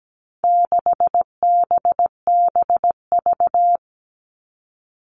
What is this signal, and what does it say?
Morse code, heard as a keyed tone, about 17 wpm, 700 Hz. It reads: 6BBV
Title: All Cut Numbers - 4-Digit 17wpm